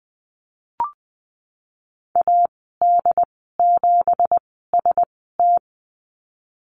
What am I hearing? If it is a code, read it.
Morse code, heard as a keyed tone, about 20 wpm, 700 Hz. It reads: A D 7 S T
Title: 231 Lesson 37g - Letter x - Call Signs 20wpm